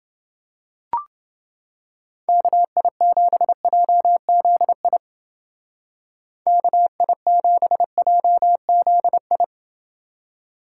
Morse code, heard as a keyed tone, about 30 wpm, 700 Hz. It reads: KI7JZI KI7JZI